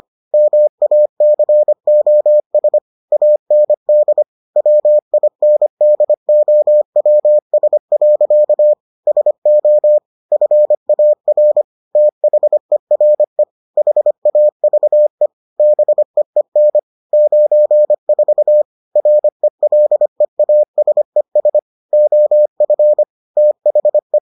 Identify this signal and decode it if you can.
Morse code, heard as a keyed tone, about 25 wpm, 600 Hz. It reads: MACOS AND WINDOWS. SO FAR THERE HAVE BEEN 94 RELEASES OF THE